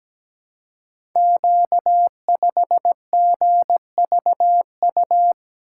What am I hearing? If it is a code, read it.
Morse code, heard as a keyed tone, about 17 wpm, 700 Hz. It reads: Q5GVU
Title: Code Groups - 5 Letter:Numbers - 17wpm